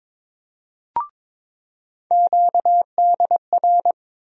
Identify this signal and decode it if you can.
Morse code, heard as a keyed tone, about 22 wpm, 700 Hz. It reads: QDR
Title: USA Call Sign Postfixes - 3 Letters 22wpm